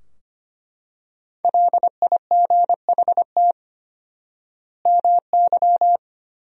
Morse code, heard as a keyed tone, about 25 wpm, 700 Hz. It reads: LIGHT MY
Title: 4x Word-Spacing - Sentences from Top 400 Words - 25wpm